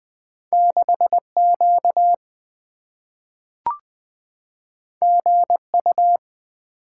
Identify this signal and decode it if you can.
Morse code, heard as a keyed tone, about 20 wpm, 700 Hz. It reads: 6Q GU
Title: Two Letter-Number Combinations - 20wpm